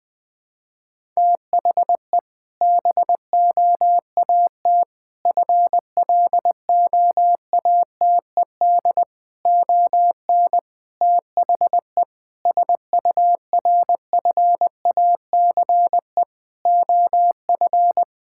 Morse code, heard as a keyed tone, about 21 wpm, 700 Hz. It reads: THE BOAT FLOATED ON THE SURFACE OF
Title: Sentences with 4-Word Phrases - Spoken Beginning - 20wpm